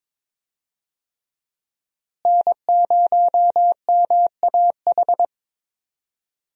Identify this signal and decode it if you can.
Morse code, heard as a keyed tone, about 22 wpm, 700 Hz. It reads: N0MAH
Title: USA Call Signs - 5 Characters - Repeated 3x - 22wpm